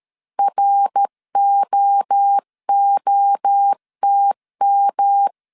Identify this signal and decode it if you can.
Morse code, heard as a keyed tone, about 13 wpm, 800 Hz. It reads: ROOTM
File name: CW / DET / 1 3 1500C S B 12wpm 2828